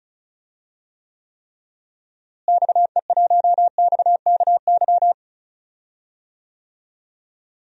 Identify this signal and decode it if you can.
Morse code, heard as a keyed tone, about 36 wpm, 700 Hz. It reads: XE1XKY
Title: DXCC-100 Call Signs - 6 Characters - Repeated 3x - 35wpm